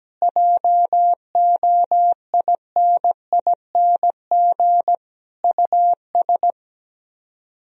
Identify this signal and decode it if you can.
Morse code, heard as a keyed tone, about 17 wpm, 700 Hz. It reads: JOINING US
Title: Speed-Racing - Sets of 5 Words 17wpm